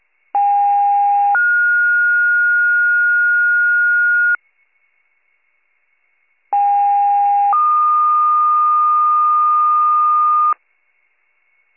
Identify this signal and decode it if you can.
Two-tone page, heard as a one-second tone, then a three-second tone, about 810 then 1430 Hz, about 810 then 1220 Hz. It is heard from 0.3 to 10.5 s.